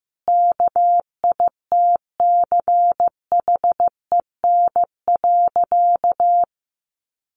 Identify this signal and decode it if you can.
Morse code, heard as a keyed tone, about 15 wpm, 700 Hz. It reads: KITCHEN.
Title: Sentences from Top 1400 Words 15wpm